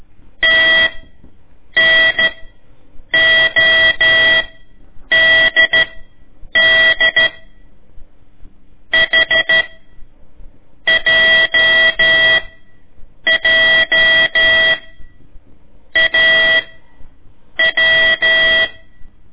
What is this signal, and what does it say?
Morse code, heard as a keyed tone, about 14 wpm, spaced as for 7 wpm, 1950 Hz. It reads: TNODD HJJAW
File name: ik1weg2 / CORSI CW / lez4-1